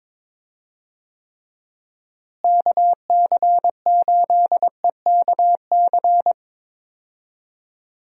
Morse code, heard as a keyed tone, about 21 wpm, 700 Hz. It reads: KC8EKC
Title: USA Call Signs - 6 Characters - Repeated 2x - 22wpm